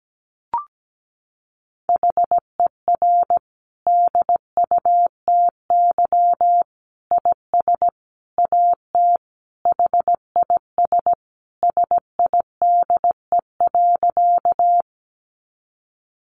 Morse code, heard as a keyed tone, about 17 wpm, 700 Hz. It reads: HER DUTY IS AT HIS SIDE.
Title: Sentences from Top 1500 Words 17wpm